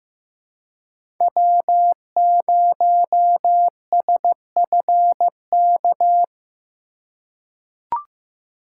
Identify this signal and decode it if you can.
Morse code, heard as a keyed tone, about 15 wpm, 700 Hz. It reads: W0SFK
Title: USA Call Signs - 4x Character Spacing - 15wpm